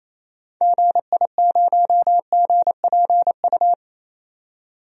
Morse code, heard as a keyed tone, about 28 wpm, 700 Hz. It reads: GI0GPU
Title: DXCC-100 Call Signs - 6 Characters - 28wpm